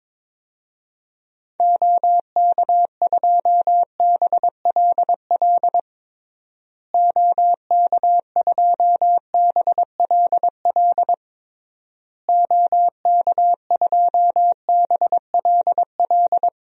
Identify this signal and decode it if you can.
Morse code, heard as a keyed tone, about 22 wpm, 700 Hz. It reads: OK2BLL OK2BLL OK2BLL
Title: DXCC-100 Call Signs - 6 Characters - Repeated 3x - 22wpm